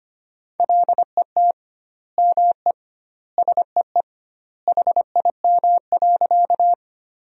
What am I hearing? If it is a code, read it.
Morse code, heard as a keyed tone, about 25 wpm, 700 Hz. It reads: LET ME SEE HIM.
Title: 2x Word-Spacing - Sentences from Top 300 Words - 25wpm